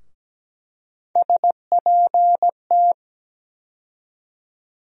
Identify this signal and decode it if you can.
Morse code, heard as a keyed tone, about 17 wpm, 700 Hz. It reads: SPT
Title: USA Call Sign Postfixes - 3 Letters 17wpm